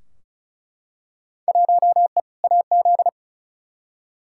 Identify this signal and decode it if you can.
Morse code, heard as a keyed tone, about 34 wpm, 700 Hz. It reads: 1E AZ
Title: Field Day Exchanges - 35wpm